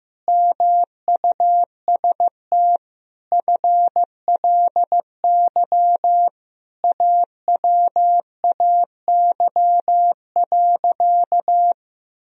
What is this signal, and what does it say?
Morse code, heard as a keyed tone, about 16 wpm, 700 Hz. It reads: MUST FLY AWAY.
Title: Sentences from Top 1000 Words - Review Part 2 - 15wpm